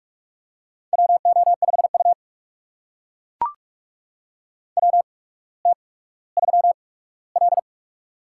Morse code, heard as a keyed tone, about 42 wpm, spaced as for 19 wpm, 700 Hz. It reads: WO5U WT3L
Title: USA Call Signs - 4 Characters - 8x Character Spacing - 45wpm